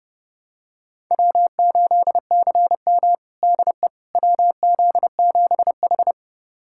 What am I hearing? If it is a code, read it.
Morse code, heard as a keyed tone, about 30 wpm, 700 Hz. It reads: W8CM DE WZ7H
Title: USA Callsign DE Callsign - 4 Characters - 30wpm